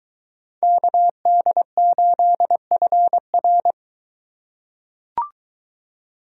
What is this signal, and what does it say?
Morse code, heard as a keyed tone, about 21 wpm, 700 Hz. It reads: KD8FR